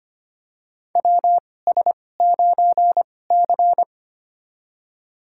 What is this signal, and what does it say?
Morse code, heard as a keyed tone, about 25 wpm, spaced as for 18 wpm, 700 Hz. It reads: WS9C
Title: USA Call Signs - 4 Characters - 2x Character Spacing - 25wpm